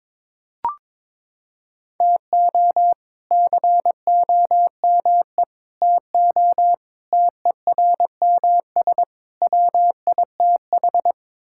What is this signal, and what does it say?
Morse code, heard as a keyed tone, about 22 wpm, 700 Hz. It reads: TO COME TO TERMS WITH
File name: practice / 22wpm / Sets of 8 Words - 22wpm